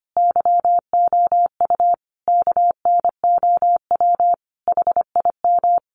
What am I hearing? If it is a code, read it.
Morse code, heard as a keyed tone, about 25 wpm, 700 Hz. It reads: YOU KNOW HIM
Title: Sentences from Top 1000 Words 25wpm